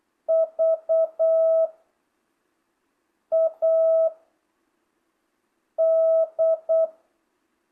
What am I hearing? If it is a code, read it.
Morse code, heard as a keyed tone, about 8 wpm, 660 Hz. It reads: V A D